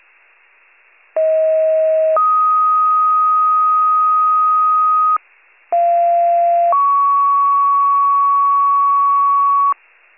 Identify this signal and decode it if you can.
Two-tone page, heard as a one-second tone, then a three-second tone, about 640 then 1250 Hz, about 690 then 1120 Hz. It is heard from 1.2 to 9.7 s.
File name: pager / 38 FIRE / wcalert